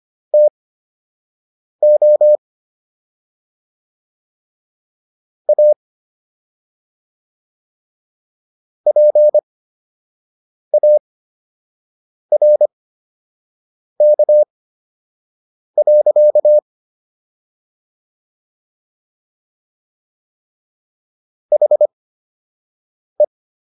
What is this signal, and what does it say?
Morse code, heard as a keyed tone, about 25 wpm, spaced as for 6 wpm, 600 Hz. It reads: TO A PARK. HE